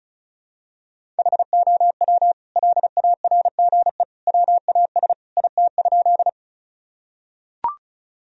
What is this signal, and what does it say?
Morse code, heard as a keyed tone, about 36 wpm, 700 Hz. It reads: HOW LARGE WAS IT?
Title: Sentences from Top 1000 Words - Review Part 1 - 35wpm